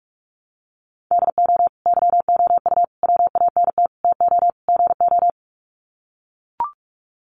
Morse code, heard as a keyed tone, about 44 wpm, 700 Hz. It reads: DO YOU WANT TO GO
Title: Sets of 5 Words Encore2 45wpm